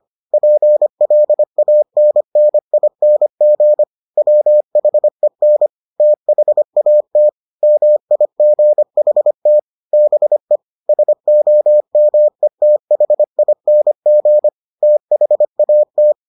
Morse code, heard as a keyed tone, about 25 wpm, 600 Hz. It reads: PLANNING WHEN THAT MIGHT BE SOMETHING THAT